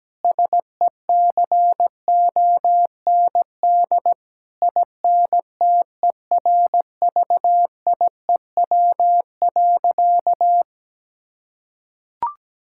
Morse code, heard as a keyed tone, about 17 wpm, 700 Hz. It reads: SECOND INTERVIEW.
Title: Sentences from Top 1600 Words 17wpm